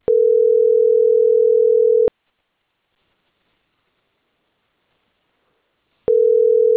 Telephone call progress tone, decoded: ringback tone